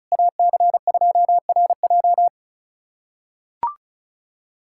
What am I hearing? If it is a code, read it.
Morse code, heard as a keyed tone, about 36 wpm, 700 Hz. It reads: AC2RJ
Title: K1USN SST QSO Practice - 35wpm